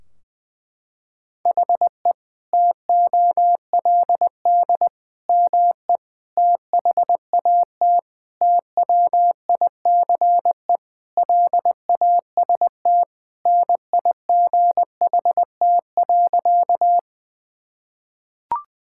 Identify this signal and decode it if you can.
Morse code, heard as a keyed tone, about 19 wpm, 700 Hz. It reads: HE TOLD ME THAT TWICE LAST NIGHT.
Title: Sentences from Top 1600 Words 20wpm